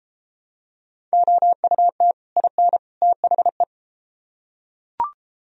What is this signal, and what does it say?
Morse code, heard as a keyed tone, about 30 wpm, 700 Hz. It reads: OUT IN THE